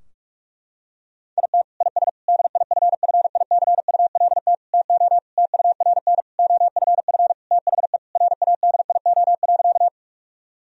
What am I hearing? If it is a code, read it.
Morse code, heard as a keyed tone, about 46 wpm, 700 Hz. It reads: IT IS DIFFICULT TO TURN OFF THE RADIO.
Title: Sentences from Top 1000 Words - Review Part 3 - 45wpm